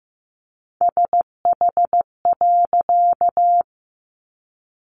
Morse code, heard as a keyed tone, about 15 wpm, 700 Hz. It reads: SH.
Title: A Sound Of Thunder 15wpm